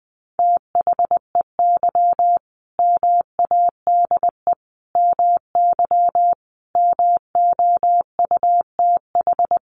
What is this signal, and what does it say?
Morse code, heard as a keyed tone, about 20 wpm, 700 Hz. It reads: THEY MADE MY MOUTH
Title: Sentences from Top 1500 Words 20wpm